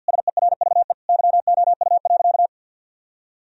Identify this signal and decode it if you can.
Morse code, heard as a keyed tone, about 50 wpm, 700 Hz. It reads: SERVE YOU.
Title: Sentences from Top 1000 Words - Review Part 2 - 50wpm